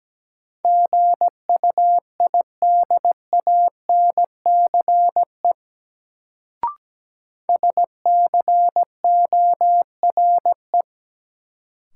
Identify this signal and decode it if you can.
Morse code, heard as a keyed tone, about 17 wpm, 700 Hz. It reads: GUIDANCE SCORE
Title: Sentences from Top 1600 Words - New Word Drill - 17wpm